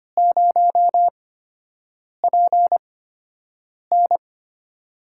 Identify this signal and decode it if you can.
Morse code, heard as a keyed tone, about 25 wpm, spaced as for 7 wpm, 700 Hz. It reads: 0PN
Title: USA Call Signs - 4 Characters - 8x Character Spacing - 25wpm